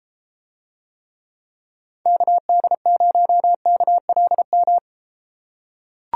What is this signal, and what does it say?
Morse code, heard as a keyed tone, about 32 wpm, 700 Hz. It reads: KD0KLM